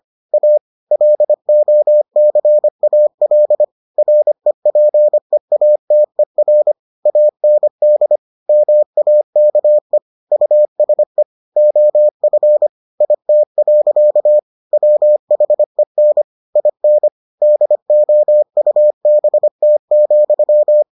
Morse code, heard as a keyed tone, about 25 wpm, 600 Hz. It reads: A LOCAL REPEATER AND MAKE USE OF IT. WHEN IN DOUBT,